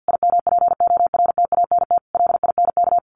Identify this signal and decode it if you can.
Morse code, heard as a keyed tone, about 50 wpm, 700 Hz. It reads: IMPORTANT LINK
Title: A Sound Of Thunder 50wpm